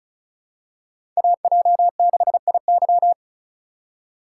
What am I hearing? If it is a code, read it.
Morse code, heard as a keyed tone, about 36 wpm, 700 Hz. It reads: AJ6IY
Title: USA Call Signs - 5 Characters - 3x Character Spacing - 35wpm